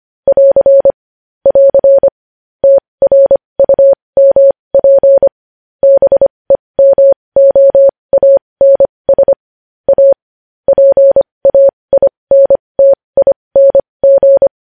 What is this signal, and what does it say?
Morse code, heard as a keyed tone, about 25 wpm, spaced as for 20 wpm, 550 Hz. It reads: <AR> <AR> TRUMP BEMOANS A PAINTING
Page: news-headlines-in-morse-code-at-20-wpm